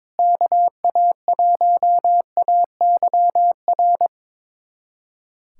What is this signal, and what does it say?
Morse code, heard as a keyed tone, about 21 wpm, 700 Hz. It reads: KA1AYR
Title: USA Callsign DE Callsign - 6 Characters - 22wpm